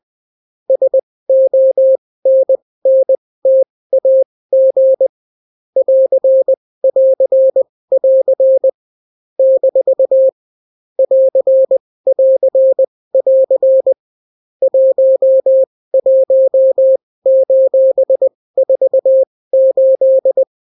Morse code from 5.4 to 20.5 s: <AR><AR><AR> - <AR><AR><AR> 11:48